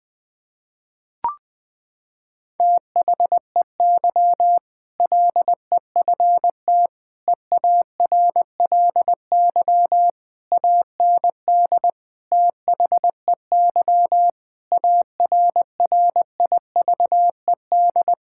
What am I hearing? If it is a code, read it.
Morse code, heard as a keyed tone, about 20 wpm, 700 Hz. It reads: THEY LEFT EARLY AND THEY ARRIVED
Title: Sentences with 2-Word Phrases - 20wpm